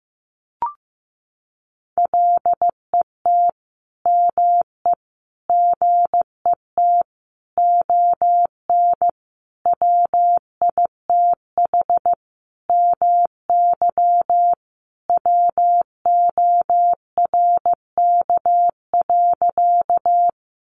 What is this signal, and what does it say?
Morse code, heard as a keyed tone, about 15 wpm, 700 Hz. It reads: LET ME GET ON WITH MY WORK.
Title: Sentences from Top 300 Words 15wpm